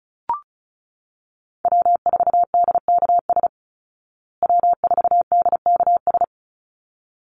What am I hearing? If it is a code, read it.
Morse code, heard as a keyed tone, about 34 wpm, 700 Hz. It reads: W4DKS W4DKS
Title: Call Signs Repeated 3x - 35wpm